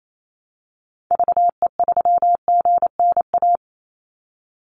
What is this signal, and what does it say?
Morse code, heard as a keyed tone, about 28 wpm, 700 Hz. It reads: VE3GNA
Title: Call Signs - 28wpm